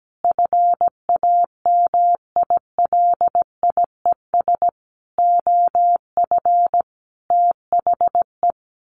Morse code, heard as a keyed tone, about 17 wpm, 700 Hz. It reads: FAMILIES OF THE
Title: A Sound Of Thunder 17wpm